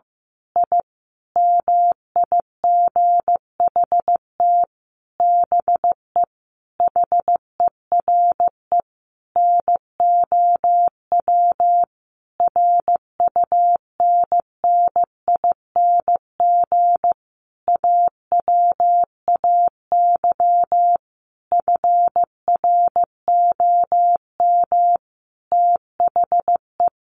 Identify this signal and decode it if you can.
Morse code, heard as a keyed tone, about 15 wpm, 700 Hz. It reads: I MIGHT BE HERE NOW RUNNING AWAY FROM THE
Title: A Sound Of Thunder 15wpm